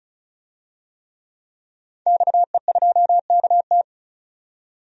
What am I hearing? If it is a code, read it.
Morse code, heard as a keyed tone, about 34 wpm, 700 Hz. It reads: XE2KT
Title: DXCC-100 Call Signs - 5 Characters - 35wpm